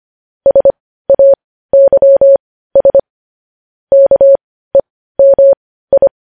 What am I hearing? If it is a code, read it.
Morse code, heard as a keyed tone, about 25 wpm, spaced as for 15 wpm, 550 Hz. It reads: SAYS KEMI
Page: news headlines in morse code at 15 wpm